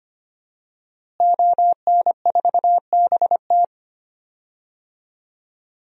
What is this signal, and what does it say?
Morse code, heard as a keyed tone, about 25 wpm, 700 Hz. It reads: ON4BT
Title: DXCC-100 Call Signs - 5 Characters - Repeated 2x - 25wpm